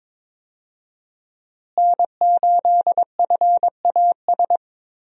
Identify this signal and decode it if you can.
Morse code, heard as a keyed tone, about 22 wpm, 700 Hz. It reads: N8FAS
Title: USA Call Signs - 5 Characters - Repeated 2x - 22wpm